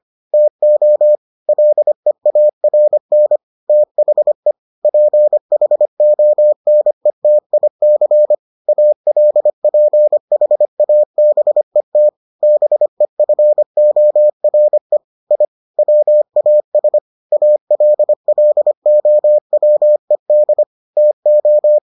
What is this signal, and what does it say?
Morse code, heard as a keyed tone, about 25 wpm, 600 Hz. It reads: TO LEARN THE PHONETIC ALPHABET BEFORE I WAS ALLOWED TO